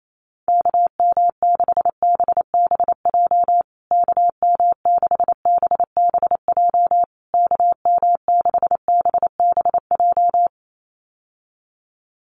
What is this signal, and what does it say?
Morse code, heard as a keyed tone, about 28 wpm, 700 Hz. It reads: KM6BBJ KM6BBJ KM6BBJ
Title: USA Call Signs Repeated 3x 28wpm